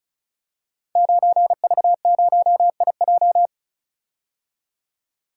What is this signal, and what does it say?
Morse code, heard as a keyed tone, about 36 wpm, 700 Hz. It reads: 9V0IJ
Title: Code Groups - 5 Letter:Numbers - 35wpm